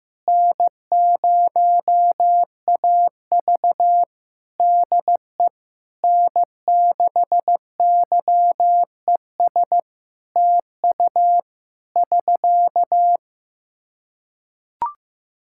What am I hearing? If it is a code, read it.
Morse code, heard as a keyed tone, about 15 wpm, 700 Hz. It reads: N0AV DE N6YES TU <SK>
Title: Practice QSO - Exchange 4 - Ending - 15wpm